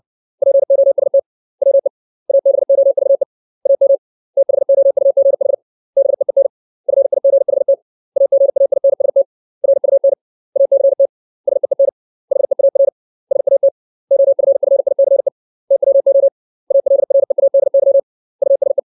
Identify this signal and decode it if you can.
Morse code, heard as a keyed tone, about 60 wpm, 550 Hz. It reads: POST WE ABOVE AM THOUGH BEEN FIGHT AGAINST RUN ACT SEEN HEAR SAT OFFICE TWO ALREADY USE
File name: 632 words / 632 Words 60WPM